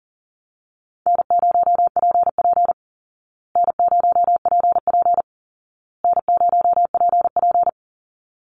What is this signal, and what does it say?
Morse code, heard as a keyed tone, about 40 wpm, 700 Hz. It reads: N0PP N0PP N0PP
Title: Call Signs Repeated 2x - 40wpm